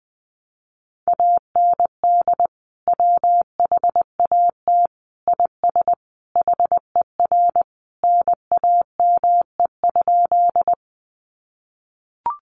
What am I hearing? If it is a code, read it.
Morse code, heard as a keyed tone, about 20 wpm, 700 Hz. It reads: AND WHAT IS HER NAME?
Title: Sentences from Top 200 Words 20wpm